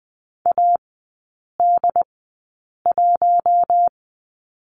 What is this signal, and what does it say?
Morse code, heard as a keyed tone, about 20 wpm, spaced as for 8 wpm, 700 Hz. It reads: AD1
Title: 067 Lesson 12e - Letter d - Three Characters 20wpm